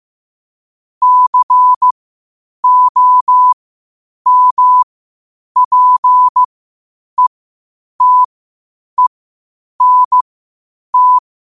Morse code, heard as a keyed tone, about 15 wpm, spaced as for 8 wpm, 1000 Hz. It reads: COMPETENT
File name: qotdfiles / 8wpm